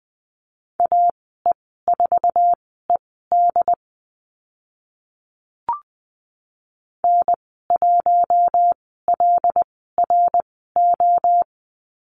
Morse code, heard as a keyed tone, about 20 wpm, spaced as for 14 wpm, 700 Hz. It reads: AE4ED N1LRO
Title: 082 Lesson 14g - Number 2 - Call Signs 20wpm